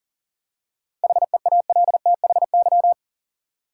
Morse code, heard as a keyed tone, about 40 wpm, 700 Hz. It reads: HEALTHY
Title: Seven Letter Words - 40wpm